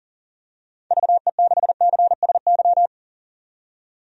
Morse code, heard as a keyed tone, about 40 wpm, 700 Hz. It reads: VE6CSY
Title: DXCC-100 Call Signs - 6 Characters - 40wpm